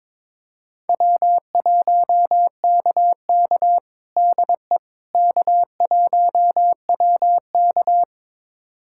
Morse code, heard as a keyed tone, about 23 wpm, 700 Hz. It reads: W1KK DE K1WK
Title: USA Callsign DE Callsign - 4 Characters - 22wpm